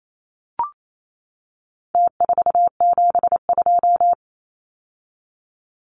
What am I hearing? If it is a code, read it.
Morse code, heard as a keyed tone, about 28 wpm, 700 Hz. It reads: T472
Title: Numbers - Contest Style - 4-Digit 28wpm